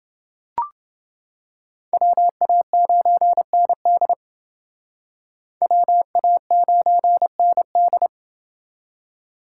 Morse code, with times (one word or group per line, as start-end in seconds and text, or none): 1.9-4.1 WA9ND
5.6-8.1 WA9ND